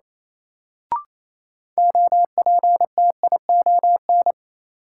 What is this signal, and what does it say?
Morse code, heard as a keyed tone, about 27 wpm, 700 Hz. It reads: OPTION
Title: Sentences from Top 1300 Words - New Word Drill - 28wpm